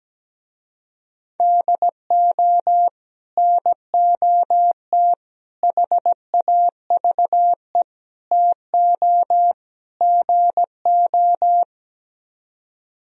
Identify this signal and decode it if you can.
Morse code, heard as a keyed tone, about 17 wpm, 700 Hz. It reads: DO NOT HAVE TO GO
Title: Speed-Racing - Sets of 5 Words 17wpm